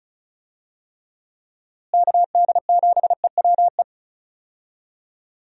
Morse code, heard as a keyed tone, about 34 wpm, 700 Hz. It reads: KD7EWE
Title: USA Call Signs - 6 Characters - Repeated 3x - 35wpm